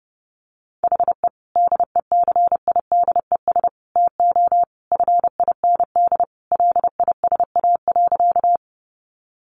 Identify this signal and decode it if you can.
Morse code, heard as a keyed tone, about 30 wpm, 700 Hz. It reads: HE DECIDES TO FIND LISA.
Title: Additional Easy Phrases 30wpm